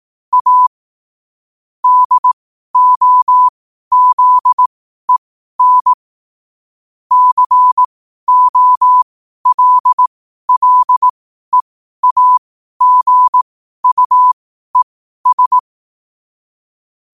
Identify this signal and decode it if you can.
Morse code, heard as a keyed tone, about 18 wpm, spaced as for 12 wpm, 1000 Hz. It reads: A DOZEN COLLEAGUES
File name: qotdfiles / 12wpm